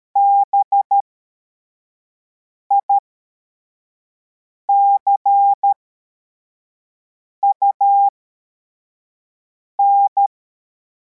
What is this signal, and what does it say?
Morse code, heard as a keyed tone, about 13 wpm, spaced as for 4 wpm, 800 Hz. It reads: BICUN
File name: formation / cw / Text61